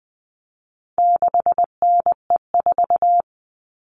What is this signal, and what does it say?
Morse code, heard as a keyed tone, about 20 wpm, 700 Hz. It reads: 6NE4
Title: All Cut Numbers - 4-Digit 20wpm